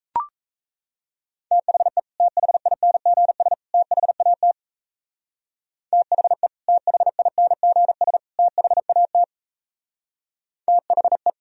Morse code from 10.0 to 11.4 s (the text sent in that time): THE